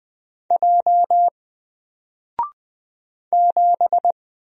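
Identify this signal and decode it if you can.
Morse code, heard as a keyed tone, about 20 wpm, 700 Hz. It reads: J 7
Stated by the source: Single letter and number Morse drill at 20 wpm, from N4LCD.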